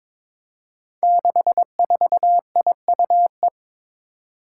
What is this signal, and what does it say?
Morse code, heard as a keyed tone, about 22 wpm, 700 Hz. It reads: 64IUE
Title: Five Challenging Letter-Number Combinations 22wpm